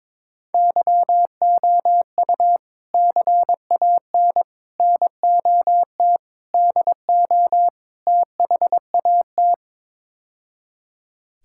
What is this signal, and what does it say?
Morse code, heard as a keyed tone, about 22 wpm, 700 Hz. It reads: YOU CAN NOT DO THAT
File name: practice / 22wpm / Build A Phrase - 5-Word Phrases - 22wpm